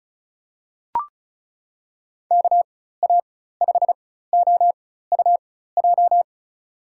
Morse code, heard as a keyed tone, about 34 wpm, spaced as for 16 wpm, 700 Hz. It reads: KA5OUJ
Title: USA Call Signs - 6 Characters - 4x Character Spacing - 35wpm